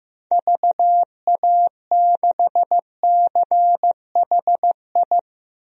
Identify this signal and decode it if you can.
Morse code, heard as a keyed tone, about 16 wpm, 700 Hz. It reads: VA6CHI
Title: DXCC-100 Call Signs - 6 Characters - Repeated 2x - 15wpm